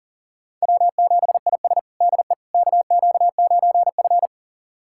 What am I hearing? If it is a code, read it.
Morse code, heard as a keyed tone, about 37 wpm, 700 Hz. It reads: W7IS DE KQ9F
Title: USA Callsign DE Callsign - 4 Characters - 40wpm